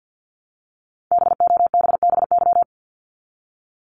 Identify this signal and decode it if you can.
Morse code, heard as a keyed tone, about 50 wpm, 700 Hz. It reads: BOBBY
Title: Multi-letter Phonemes Repeated 3x 50wpm